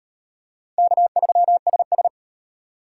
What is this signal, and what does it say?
Morse code, heard as a keyed tone, about 31 wpm, 700 Hz. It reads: K3SS